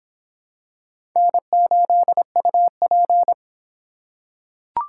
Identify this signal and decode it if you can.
Morse code, heard as a keyed tone, about 23 wpm, 700 Hz. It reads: N8UP